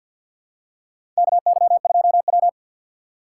Morse code, heard as a keyed tone, about 50 wpm, 700 Hz. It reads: KY2W
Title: USA Call Signs - 4x Character Spacing - 50wpm